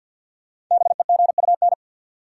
Morse code, heard as a keyed tone, about 53 wpm, 700 Hz. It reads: BEGUN
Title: Sentences from Top 1000 Words - Review Part 6 - New Word Drill - 50wpm